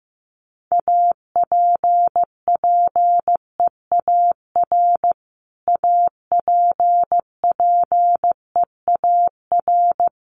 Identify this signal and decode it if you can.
Morse code, heard as a keyed tone, about 15 wpm, 700 Hz. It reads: APPEAR APPEAR
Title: Top 500 Words Repeated 15wpm